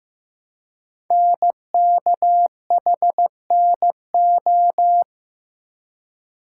Morse code, heard as a keyed tone, about 15 wpm, 700 Hz. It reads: NKHNO
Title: Code Groups - 5 Letters - 15wpm